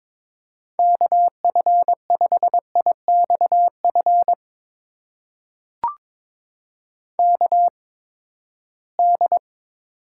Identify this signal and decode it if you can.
Morse code, heard as a keyed tone, about 22 wpm, spaced as for 15 wpm, 700 Hz. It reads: KF5IXF KD
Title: USA Call Signs - 6 Characters - 8x Character Spacing - 22wpm